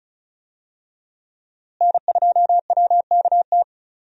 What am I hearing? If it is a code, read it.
Morse code, heard as a keyed tone, about 36 wpm, 700 Hz. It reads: N2WKT
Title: USA Call Signs - 5 Characters - Repeated 3x - 35wpm